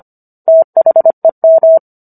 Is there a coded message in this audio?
Yes: Morse code from 0.5 to 1.8 s, heard as a keyed tone, about 25 wpm, 650 Hz. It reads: THEM